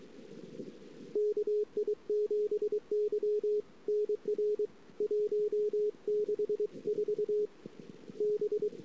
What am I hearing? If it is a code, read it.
Morse code, heard as a keyed tone, about 23 wpm, 420 Hz. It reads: KI7Y NR 164 B
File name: ss-cw